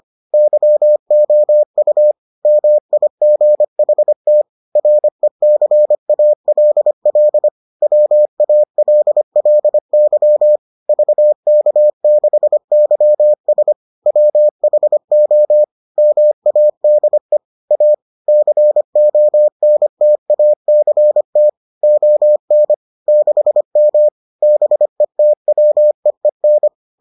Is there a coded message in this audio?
Yes: Morse code from 0.3 to 26.7 s, heard as a keyed tone, about 25 wpm, 600 Hz. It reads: YOU MIGHT RECALL WALLY VK6YS WHO MADE A CONTACT ON 6M BETWEEN